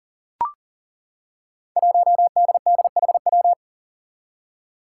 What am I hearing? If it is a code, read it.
Morse code, heard as a keyed tone, about 40 wpm, 700 Hz. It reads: 1DDHW
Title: Five Challenging Letter-Number Combinations 40wpm